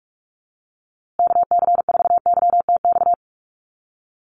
Morse code, heard as a keyed tone, about 44 wpm, 700 Hz. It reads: KC4YTX
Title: USA Call Signs Repeated 3x 45wpm